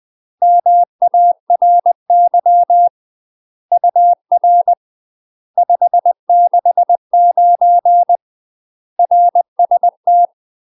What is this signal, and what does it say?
Morse code, heard as a keyed tone, about 20 wpm, 700 Hz. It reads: MARY UR 569 RST
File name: UK MASTER QSOs TRAINING / 20WPM / UK MASTER QSOs 0049